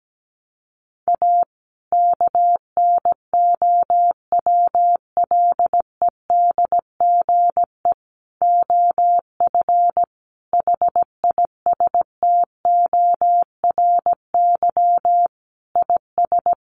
Morse code, heard as a keyed tone, about 17 wpm, 700 Hz. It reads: A KNOWLEDGE OF HISTORY IS
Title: Sentences from Top 900 Words 17wpm